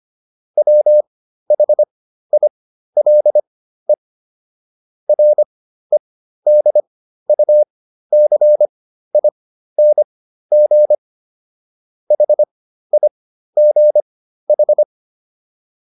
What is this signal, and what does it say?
Morse code, heard as a keyed tone, about 25 wpm, spaced as for 13 wpm, 600 Hz. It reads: WHILE REDUCING HIGH